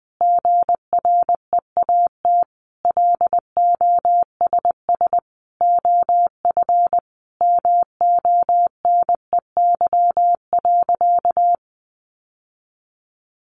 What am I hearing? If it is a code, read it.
Morse code, heard as a keyed tone, about 20 wpm, 700 Hz. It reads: GREAT LOSS OF MONEY.
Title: Sentences from Top 1400 Words 20wpm